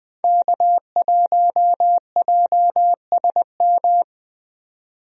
Morse code, heard as a keyed tone, about 20 wpm, 700 Hz. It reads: K1JSM
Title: USA Call Signs - 5 Characters - 20wpm